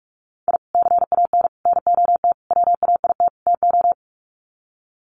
Morse code, heard as a keyed tone, about 44 wpm, 700 Hz. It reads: I CAN NOT WAIT TO
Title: Sets of 5 Words Encore 45wpm